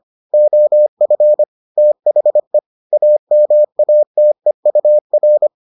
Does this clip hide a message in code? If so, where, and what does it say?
Yes: Morse code from 0.3 to 5.5 s, heard as a keyed tone, about 25 wpm, 600 Hz. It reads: OF THE AMATEUR